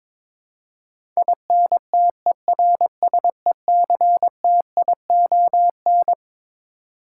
Morse code, heard as a keyed tone, about 22 wpm, 700 Hz. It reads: INTERSECTION
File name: practice / 22wpm / Twelve Letter Words - 22wpm